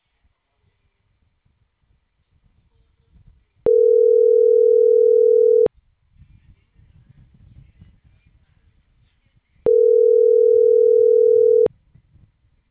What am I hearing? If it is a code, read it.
Ringback tone (North America), 440 + 480 Hz.
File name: CENTRAL CAEA / out agcaea01 - .26231